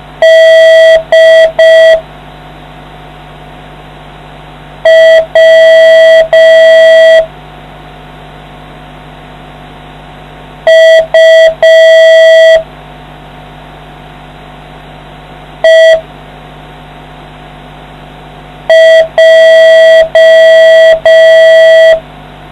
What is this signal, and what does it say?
Morse code, heard as a keyed tone, about 5 wpm, 650 Hz. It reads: D W U E J